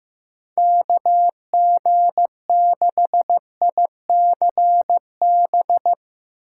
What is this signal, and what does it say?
Morse code, heard as a keyed tone, about 15 wpm, 700 Hz. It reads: KG6ICB
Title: USA Call Signs - 6 Characters - 4x Character Spacing - 15wpm